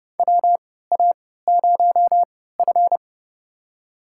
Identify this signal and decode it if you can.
Morse code, heard as a keyed tone, about 30 wpm, spaced as for 17 wpm, 700 Hz. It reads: WA0F
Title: USA Call Signs - 3x Character Spacing - 30wpm